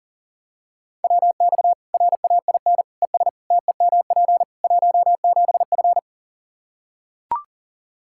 Morse code, heard as a keyed tone, about 40 wpm, 700 Hz. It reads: WX RAIN ES TEMP 17F
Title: Practice QSO - Exchange 2 - Info - 40wpm